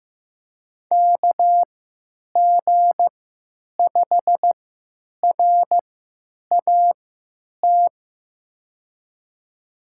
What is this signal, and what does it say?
Morse code, heard as a keyed tone, about 15 wpm, spaced as for 9 wpm, 700 Hz. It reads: KG5RAT
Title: USA Call Signs - 6 Characters - 3x Character Spacing - 15wpm